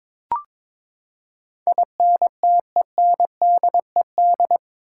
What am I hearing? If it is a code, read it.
Morse code, heard as a keyed tone, about 22 wpm, 700 Hz. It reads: INTENDED
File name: practice / 22wpm / Eight Letter Words - 22wpm